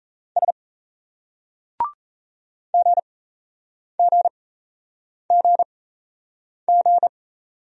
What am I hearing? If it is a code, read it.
Morse code, heard as a keyed tone, about 35 wpm, spaced as for 8 wpm, 700 Hz. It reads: S GGGG
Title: Speed-Racing Single Letter-Number 28wpm